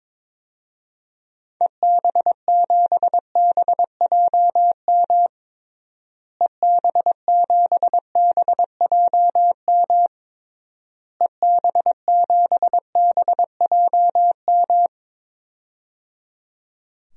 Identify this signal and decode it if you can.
Morse code, heard as a keyed tone, about 22 wpm, 700 Hz. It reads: EB7BJM EB7BJM EB7BJM
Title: DXCC-100 Call Signs - 6 Characters - Repeated 3x - 22wpm